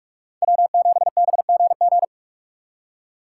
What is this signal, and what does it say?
Morse code, heard as a keyed tone, about 45 wpm, 700 Hz. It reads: W7BGG
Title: USA Call Signs - 5 Characters - 2x Character Spacing - 45wpm